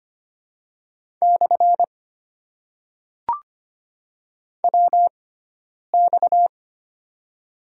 Morse code, heard as a keyed tone, about 25 wpm, spaced as for 9 wpm, 700 Hz. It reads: / WX
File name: Top 90 QSO Elements - 6x Character Spacing - 25wpm